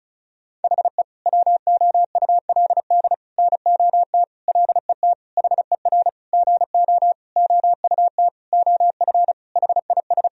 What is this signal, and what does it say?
Morse code, heard as a keyed tone, about 34 wpm, 700 Hz. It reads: HE WOULD NOT LET HER GO OUT OF HIS
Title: Sentences from Top 1000 Words - Review Part 2 - 35wpm